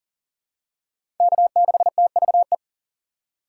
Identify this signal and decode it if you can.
Morse code, heard as a keyed tone, about 38 wpm, 700 Hz. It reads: K6TVE
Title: USA Call Signs - 5 Characters - 3x Character Spacing - 40wpm